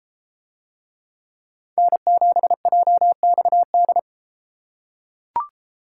Morse code, heard as a keyed tone, about 32 wpm, 700 Hz. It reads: N7JXD